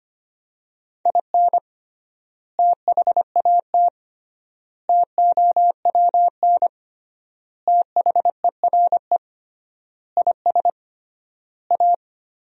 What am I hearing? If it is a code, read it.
Morse code, heard as a keyed tone, about 25 wpm, 700 Hz. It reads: IN THAT TOWN THERE IS A
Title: 3x Word-Spacing - Sentences from Top 500 Words - 25wpm